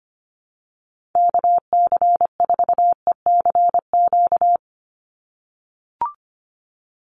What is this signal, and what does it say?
Morse code, heard as a keyed tone, about 25 wpm, 700 Hz. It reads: KC4ECQ
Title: USA Call Signs 25wpm